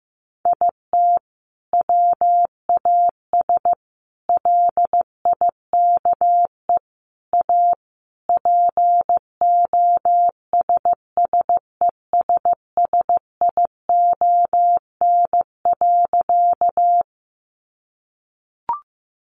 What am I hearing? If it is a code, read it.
Morse code, heard as a keyed tone, about 15 wpm, 700 Hz. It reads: IT WAS LIKE A POSSESSION.
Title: Sentences from Top 1400 Words 15wpm